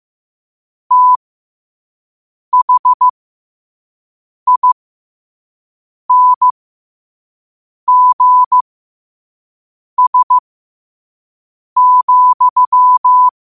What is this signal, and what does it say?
Morse code, heard as a keyed tone, about 15 wpm, spaced as for 5 wpm, 1000 Hz. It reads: THINGS,
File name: qotdfiles / 5wpm